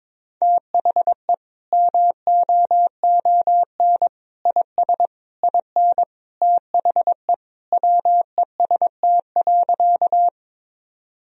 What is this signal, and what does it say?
Morse code, heard as a keyed tone, about 22 wpm, 700 Hz. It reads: THE MOON IS IN THE WEST.
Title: Sentences from Top 1000 Words - Review Part 3 - 22wpm